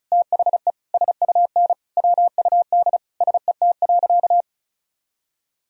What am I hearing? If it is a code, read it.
Morse code, heard as a keyed tone, about 34 wpm, 700 Hz. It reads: THE SUN WUD SET.
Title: Sentences from Top 400 Words and Abbreviations 35wpm